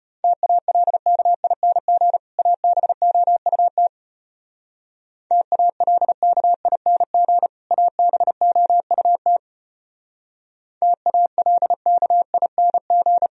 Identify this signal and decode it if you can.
Morse code, heard as a keyed tone, about 31 wpm, 700 Hz. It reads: TALKING ABOUT TALKING ABOUT TALKING
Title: Speed-Racing - Sets of 2 Words 25wpm